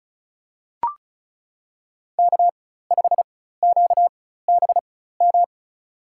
Morse code, heard as a keyed tone, about 36 wpm, spaced as for 16 wpm, 700 Hz. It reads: K5QBM
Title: USA Call Signs - 5 Characters - 4x Character Spacing - 35wpm